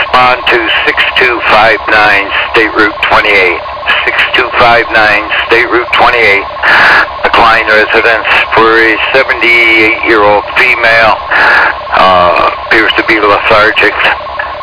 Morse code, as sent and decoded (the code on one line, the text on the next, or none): none